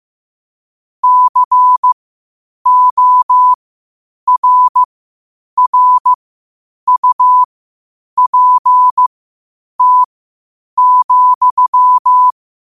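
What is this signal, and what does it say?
Morse code, heard as a keyed tone, about 15 wpm, spaced as for 8 wpm, 1000 Hz. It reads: CORRUPT,
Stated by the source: Morse code at 8 wpm.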